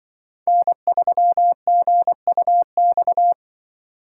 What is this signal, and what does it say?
Morse code, heard as a keyed tone, about 24 wpm, 700 Hz. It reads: N3GUX
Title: Speed Racing - USA Call Signs 20wpm